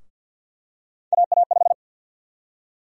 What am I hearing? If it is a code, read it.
Morse code, heard as a keyed tone, about 48 wpm, 700 Hz. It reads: AA5
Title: USA Call Sign Prefixes - Repeated 2x - 50wpm